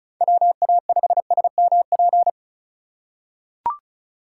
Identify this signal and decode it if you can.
Morse code, heard as a keyed tone, about 36 wpm, 700 Hz. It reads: WA5SMP
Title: USA Call Signs - 6 Characters - 4x Character Spacing - 35wpm